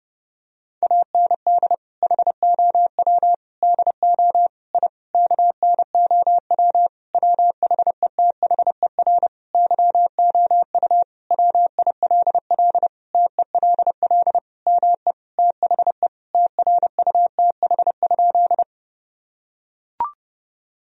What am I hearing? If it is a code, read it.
Morse code, heard as a keyed tone, about 30 wpm, 700 Hz. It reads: AND HOW DO I KNOW WHETHER YOU WILL TELL ME THE TRUTH?
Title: Sentences from Top 1000 Words - Review Part 3 - 30wpm